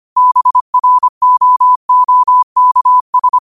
Morse code, heard as a keyed tone, about 25 wpm, 1000 Hz. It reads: BROOKS